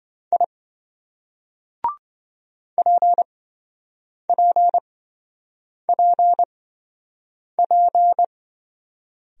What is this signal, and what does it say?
Morse code, heard as a keyed tone, about 25 wpm, spaced as for 7 wpm, 700 Hz. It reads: I PPPP